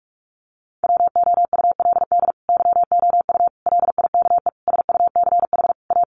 Morse code, heard as a keyed tone, about 44 wpm, 700 Hz. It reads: WOULD YOU LIKE SUCH A
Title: Sentences from Top 1000 Words 45wpm